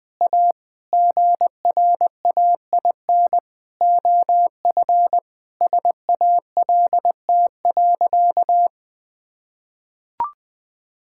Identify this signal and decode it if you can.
Morse code, heard as a keyed tone, about 20 wpm, 700 Hz. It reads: A GRAIN OF SALT.
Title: Sentences with 8-Word Phrases - 20wpm